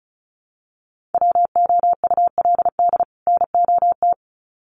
Morse code, heard as a keyed tone, about 34 wpm, 700 Hz. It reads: WOULD NOT
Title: Sets of 2 Words 35wpm